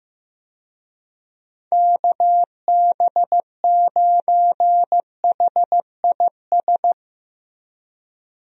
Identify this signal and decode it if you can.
Morse code, heard as a keyed tone, about 15 wpm, 700 Hz. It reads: KB9HIS
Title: USA Call Signs - 6 Characters - Repeated 2x - 15wpm